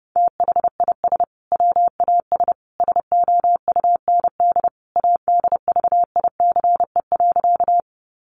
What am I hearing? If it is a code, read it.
Morse code, heard as a keyed tone, about 30 wpm, 700 Hz. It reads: THIS WAS SOUND ADVICE.
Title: Sentences from Top 1100 Words 30wpm